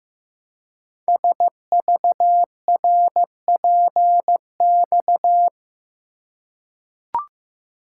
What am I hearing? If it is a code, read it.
Morse code, heard as a keyed tone, about 15 wpm, 700 Hz. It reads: SVRPX
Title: Code Groups - 5 Letters - 15wpm